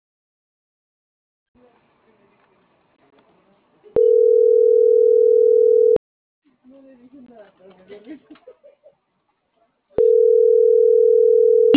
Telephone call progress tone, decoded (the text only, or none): ringback tone